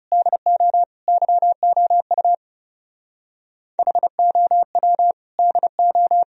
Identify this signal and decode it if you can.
Morse code, heard as a keyed tone, about 28 wpm, 700 Hz. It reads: DO YOU HOW DO
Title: Speed-Racing - Sets of 3 Words 30wpm